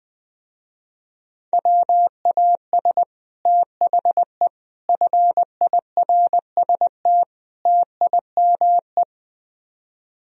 Morse code, heard as a keyed tone, about 20 wpm, 700 Hz. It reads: WAS THE FIRST TIME